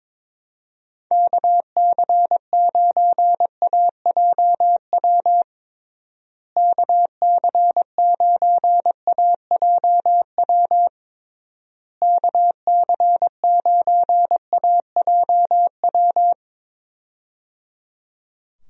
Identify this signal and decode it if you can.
Morse code, heard as a keyed tone, about 22 wpm, 700 Hz. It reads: KC9AJW KC9AJW KC9AJW
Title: USA Call Signs - 6 Characters - Repeated 3x - 22wpm